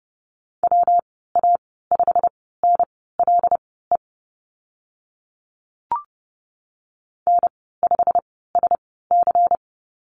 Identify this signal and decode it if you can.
Morse code, heard as a keyed tone, about 30 wpm, spaced as for 17 wpm, 700 Hz. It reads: WA5NLE N5SC
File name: Lesson 19g - Letter w - Call Signs 30wpm